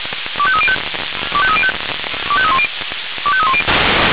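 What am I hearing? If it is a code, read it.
Five-tone selective call (ZVEI), heard as quick runs of five tones, tones of about 80 ms. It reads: 25305 25305 25310 25310
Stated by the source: Five-tone ZVEI selective call received by SDR.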